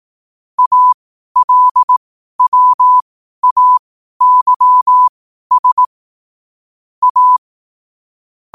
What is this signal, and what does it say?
Morse code, heard as a keyed tone, about 18 wpm, spaced as for 12 wpm, 1000 Hz. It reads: ALWAYS A
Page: qotd-at-12-wpm